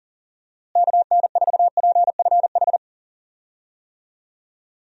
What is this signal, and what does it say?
Morse code, heard as a keyed tone, about 37 wpm, 700 Hz. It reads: KN4PFH
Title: USA Call Signs - 6 Characters - Repeated 3x - 40wpm